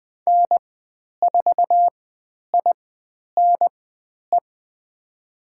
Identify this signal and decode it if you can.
Morse code, heard as a keyed tone, about 20 wpm, spaced as for 10 wpm, 700 Hz. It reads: N4INE
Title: 102 Lesson 17g - Letter u - Call Signs 20wpm